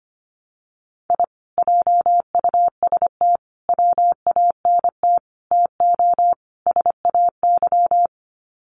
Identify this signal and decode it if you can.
Morse code, heard as a keyed tone, about 25 wpm, 700 Hz. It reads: I JUST WANT TO SAY
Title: Sets of 5 Words Encore 25wpm